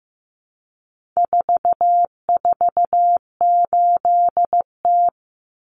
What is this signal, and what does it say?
Morse code, heard as a keyed tone, about 15 wpm, 700 Hz. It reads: 448T
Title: Common Cut Numbers - 4-Digit 15wpm